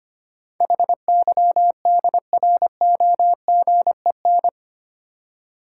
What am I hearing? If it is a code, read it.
Morse code, heard as a keyed tone, about 25 wpm, 700 Hz. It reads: HYDROGEN
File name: 8 Letter Words - First 3 Letters Given - 25wpm